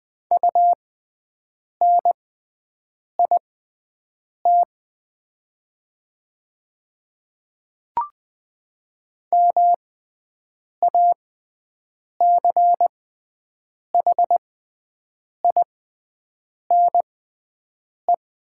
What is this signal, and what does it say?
Morse code, heard as a keyed tone, about 20 wpm, spaced as for 7 wpm, 700 Hz. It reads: UNIT MACHINE
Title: 6x Character Spacing - Top 500 Words - 20wpm